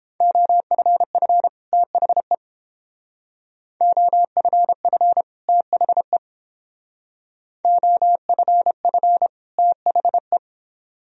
Morse code, heard as a keyed tone, about 30 wpm, 700 Hz. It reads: OFF THE OFF THE OFF THE